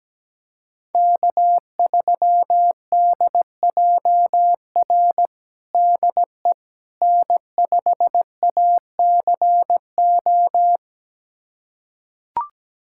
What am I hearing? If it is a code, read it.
Morse code, heard as a keyed tone, about 17 wpm, 700 Hz. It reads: K3DJR DE N5ACO
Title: USA Callsign DE Callsign - 5 Characters - 17wpm